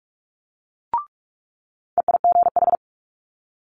Morse code, heard as a keyed tone, about 46 wpm, 700 Hz. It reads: EIGH
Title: Multi-letter Phonemes Quick Fire 45wpm